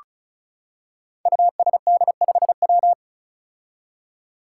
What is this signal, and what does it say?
Morse code, heard as a keyed tone, about 36 wpm, 700 Hz. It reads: USD5W
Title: Five Challenging Letter-Number Combinations 35wpm